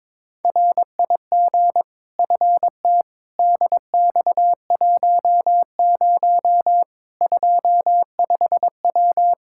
Morse code, heard as a keyed tone, about 22 wpm, 700 Hz. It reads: RIG FT DX10 25W
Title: K1USN SST QSO Practice 22wpm